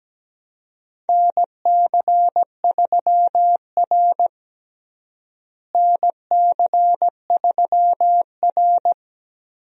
Morse code, heard as a keyed tone, about 17 wpm, 700 Hz. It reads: NC3R NC3R
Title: USA Call Signs - 4 Characters - Repeated 2x - 17wpm